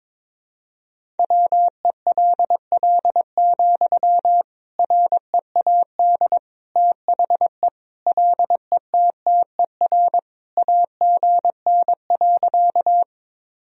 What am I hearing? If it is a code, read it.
Morse code, heard as a keyed tone, about 22 wpm, 700 Hz. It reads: WELL, READ THE LETTER AGN.
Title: Sentences from Top 300 Words and Abbreviations 22wpm